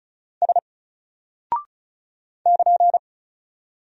Morse code, heard as a keyed tone, about 33 wpm, 700 Hz. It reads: S <KN>
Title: Single Character (Letter:Number:Punctuation:Prosign) Alt 35wpm